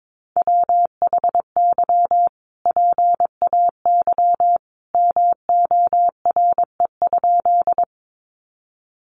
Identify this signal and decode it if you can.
Morse code, heard as a keyed tone, about 22 wpm, 700 Hz. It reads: WHY PAY MORE?
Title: Sentences from Top 700 Words 22wpm